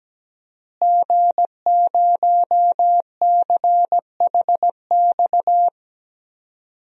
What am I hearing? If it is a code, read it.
Morse code, heard as a keyed tone, about 17 wpm, 700 Hz. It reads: G0CHX
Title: DXCC-100 Call Signs - 5 Characters - Repeated 2x - 17wpm